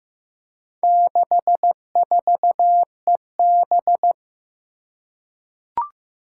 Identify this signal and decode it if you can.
Morse code, heard as a keyed tone, about 15 wpm, 700 Hz. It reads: 64EB
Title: Four Challenging Letter-Number Combinations 15wpm